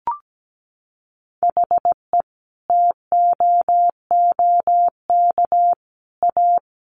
Morse code, heard as a keyed tone, about 17 wpm, 700 Hz. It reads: HE TOOK A
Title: Sentences from Top 1000 Words 17wpm